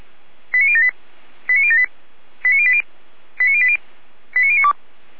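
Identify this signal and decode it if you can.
Five-tone selective call (ZVEI), heard as quick runs of five tones, tones of about 70 ms. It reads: 89078 89078 89080 89080 89092
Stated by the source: Received on an SDR, ZVEI five-tone code 89072.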